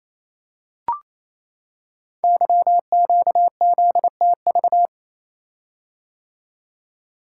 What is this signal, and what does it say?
Morse code, heard as a keyed tone, about 27 wpm, 700 Hz. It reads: YQZTV
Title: Code Groups - 5 Letter:Numbers - 28wpm